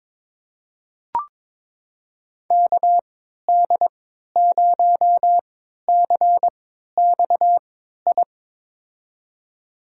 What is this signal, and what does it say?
Morse code, heard as a keyed tone, about 22 wpm, spaced as for 13 wpm, 700 Hz. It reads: KD0CXI
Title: USA Call Signs - 6 Characters - 3x Character Spacing - 22wpm